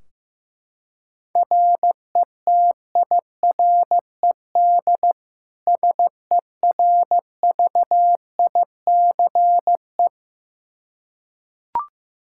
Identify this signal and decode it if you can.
Morse code, heard as a keyed tone, about 16 wpm, 700 Hz. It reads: RETIRED SERVICE
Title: Practice QSO - Exchange 3 - More Info - 15wpm